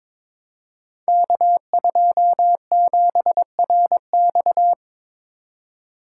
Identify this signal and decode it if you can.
Morse code, heard as a keyed tone, about 22 wpm, 700 Hz. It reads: K27RX
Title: Code Groups - 5 Letter:Numbers - 22wpm